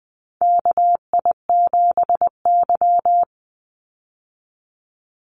Morse code, heard as a keyed tone, about 20 wpm, 700 Hz. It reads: KI7Y
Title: Call Signs Repeated 2x - 20wpm